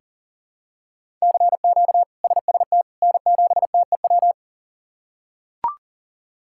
Morse code, heard as a keyed tone, about 40 wpm, 700 Hz. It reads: CQ SST N7TEW
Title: K1USN SST QSO Practice - 40wpm